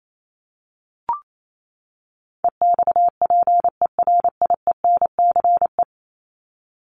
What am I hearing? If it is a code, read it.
Morse code, heard as a keyed tone, about 28 wpm, 700 Hz. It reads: EXPERIENCE
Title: Top 501-750 Words 28wpm